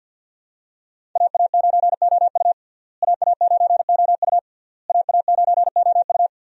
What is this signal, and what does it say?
Morse code, heard as a keyed tone, about 52 wpm, 690 Hz. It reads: AA9OU AA9OU AA9OU
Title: USA Call Signs - 5 Characters - Repeated 3x - 50wpm